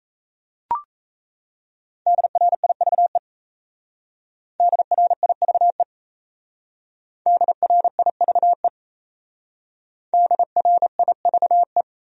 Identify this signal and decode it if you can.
Morse code, heard as a keyed tone, about 34 wpm, 700 Hz. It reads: DRIVE DRIVE DRIVE DRIVE
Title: Speed-Racing Top 500 Words 28wpm